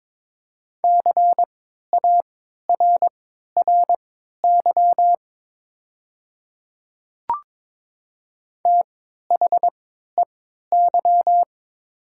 Morse code, heard as a keyed tone, about 22 wpm, spaced as for 13 wpm, 700 Hz. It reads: CARRY THEY
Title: 3x Character Spacing - Top 300 Words - 22wpm